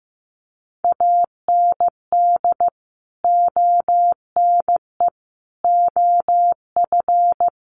Morse code, heard as a keyed tone, about 15 wpm, 700 Hz. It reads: AND ONE OF
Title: Sets of 5 Words 15wpm